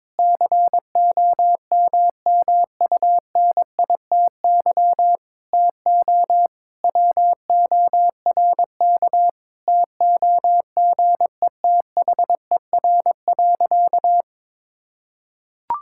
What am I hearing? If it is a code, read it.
Morse code, heard as a keyed tone, about 22 wpm, 700 Hz. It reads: COMMUNITY TO WORK TOGETHER.
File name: Sentences with 7-Word Phrases - 22wpm